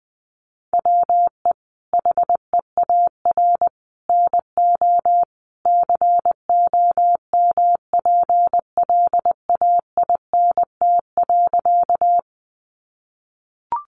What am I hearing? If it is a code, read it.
Morse code, heard as a keyed tone, about 20 wpm, 700 Hz. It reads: WE HEAR NO COMPLAINT.
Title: Sentences from Top 1400 Words 20wpm